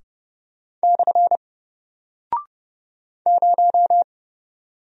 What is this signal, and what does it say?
Morse code, heard as a keyed tone, about 30 wpm, 700 Hz. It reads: / 0
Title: Single Character (Letter:Number:Punctuation:Prosign) Alt 30wpm